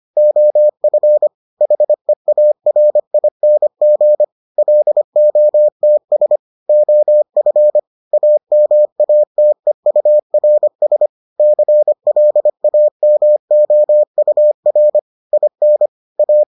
Morse code, heard as a keyed tone, about 25 wpm, 600 Hz. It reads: OF HEARING LOTS OF AMATEURS CLAMOUR IN A